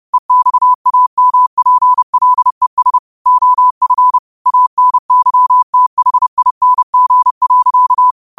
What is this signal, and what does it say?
Morse code, heard as a keyed tone, about 30 wpm, 1000 Hz. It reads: EXAMPLES OF ANYTHING.